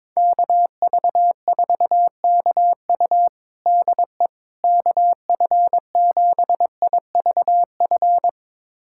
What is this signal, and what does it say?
Morse code, heard as a keyed tone, about 22 wpm, 700 Hz. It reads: KV4KU DE KF7IVF
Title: USA Callsign DE Callsign 22wpm